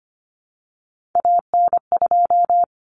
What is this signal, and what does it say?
Morse code, heard as a keyed tone, about 25 wpm, 700 Hz. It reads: AN2
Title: Common Cut Numbers - 3-Digit 25wpm